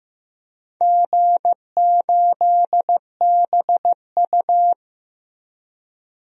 Morse code, heard as a keyed tone, about 15 wpm, 700 Hz. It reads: G8BU
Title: DXCC-100 Call Signs - 4 Characters - Repeated 3x - 15wpm